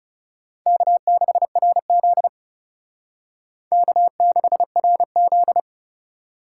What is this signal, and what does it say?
Morse code, heard as a keyed tone, about 39 wpm, 700 Hz. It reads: K6RZ K6RZ